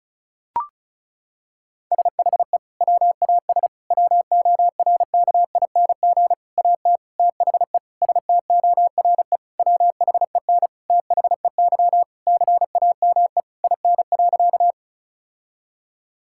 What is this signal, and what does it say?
Morse code, heard as a keyed tone, about 36 wpm, 700 Hz. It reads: SHE WAS WORKING AT THE STORE WHEN THEY CAME IN.
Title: Sentences from Top 1000 Words - Review Part 5 - 35wpm